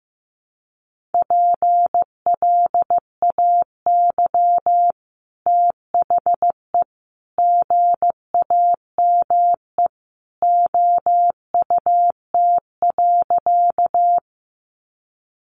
Morse code, heard as a keyed tone, about 15 wpm, 700 Hz. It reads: PLAY THE GAME OUT.
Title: Sentences from Top 500 Words 15wpm